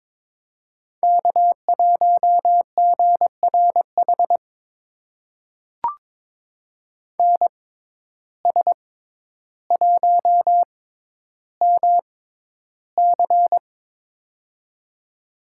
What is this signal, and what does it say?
Morse code, heard as a keyed tone, about 22 wpm, spaced as for 11 wpm, 700 Hz. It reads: K1GRH NS1MC
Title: USA Call Signs - 5 Characters - 6x Character Spacing - 22wpm